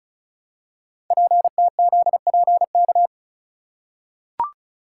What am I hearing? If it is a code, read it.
Morse code, heard as a keyed tone, about 36 wpm, 700 Hz. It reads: PTZPK
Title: Code Groups - 5 Letters - 35wpm